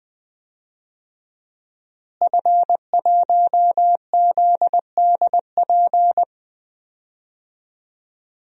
Morse code, heard as a keyed tone, about 20 wpm, 700 Hz. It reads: F1ZDP
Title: DXCC-100 Call Signs - 5 Characters - Repeated 3x - 20wpm